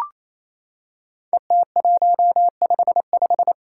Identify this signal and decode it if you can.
Morse code, heard as a keyed tone, about 29 wpm, 700 Hz. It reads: ET155
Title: Five Challenging Letter-Number Combinations 28wpm